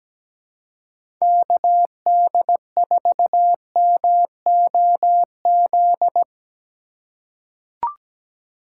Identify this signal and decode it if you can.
Morse code, heard as a keyed tone, about 17 wpm, 700 Hz. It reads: KD4MOZ
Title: USA Call Signs - 6 Characters - 4x Character Spacing - 17wpm